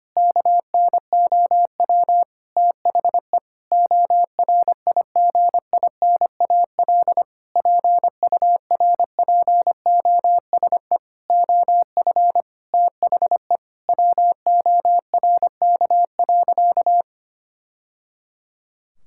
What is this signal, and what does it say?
Morse code, heard as a keyed tone, about 25 wpm, 700 Hz. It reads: KNOW THE ORIGINAL PURPOSE OF THE WORK.
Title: Sentences from Top 1000 Words - Review Part 6 - 25wpm